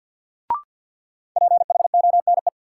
Morse code, heard as a keyed tone, about 51 wpm, 700 Hz. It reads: PHONE
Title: Sentences from Top 1100 Words - New Word Drill Alt - 50wpm